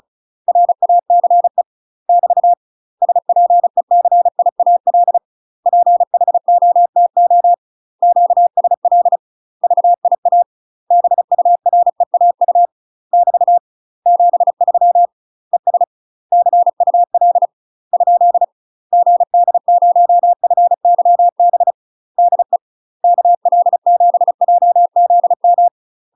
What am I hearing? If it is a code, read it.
Morse code, heard as a keyed tone, about 34 wpm, 700 Hz. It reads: RACE <BT> SPECIAL PHOTO QSL VIA BUREAU <BT> 73 ES CUL ? GD0FYB DE KL7JZM
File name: UK MASTER QSOs TRAINING / 35WPM / UK MASTER QSOs 0021